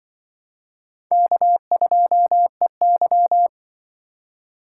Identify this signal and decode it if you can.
Morse code, heard as a keyed tone, about 25 wpm, 700 Hz. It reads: K2EY
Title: Speed Racing - USA 1x2 Call Signs 20wpm